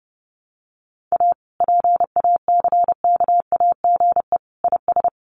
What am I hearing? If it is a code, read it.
Morse code, heard as a keyed tone, about 30 wpm, 700 Hz. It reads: A PACKAGE IS
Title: Additional Easy Phrases 30wpm